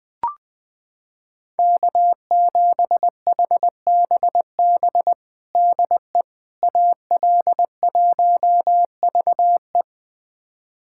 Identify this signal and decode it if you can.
Morse code, heard as a keyed tone, about 20 wpm, 700 Hz. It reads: K7HBB DE AL1VE
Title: USA Callsign DE Callsign - 5 Characters - 20wpm